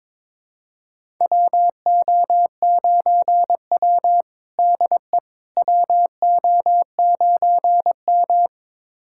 Morse code, heard as a keyed tone, about 22 wpm, 700 Hz. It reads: WO9W DE WO9M
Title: USA Callsign DE Callsign - 4 Characters - 22wpm